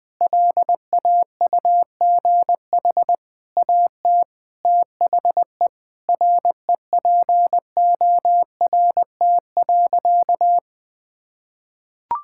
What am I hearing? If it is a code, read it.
Morse code, heard as a keyed tone, about 21 wpm, 700 Hz. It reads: LAUGH AT THE REPORT.
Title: Sentences from Top 1000 Words - Review Part 6 - 20wpm